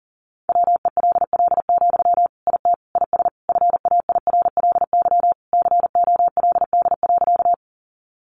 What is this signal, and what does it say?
Morse code, heard as a keyed tone, about 40 wpm, 700 Hz. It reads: WELL, IT IS FAIRLY COLD.
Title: Sentences from Top 1100 Words 40wpm